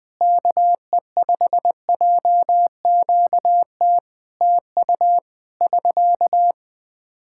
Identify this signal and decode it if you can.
Morse code, heard as a keyed tone, about 21 wpm, 700 Hz. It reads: KE5JQT TU <SK>
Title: Practice QSO - Exchange 4 - Ending - 20wpm